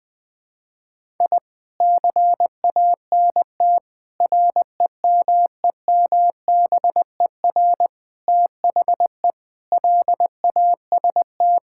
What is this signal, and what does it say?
Morse code, heard as a keyed tone, about 20 wpm, 700 Hz. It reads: I CANT REMEMBER THE LAST
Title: Sentences with 7-Word Phrases - 20wpm